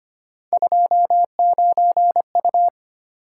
Morse code, heard as a keyed tone, about 25 wpm, 700 Hz. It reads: 29U
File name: Three Letter-Number Combinations - 25wpm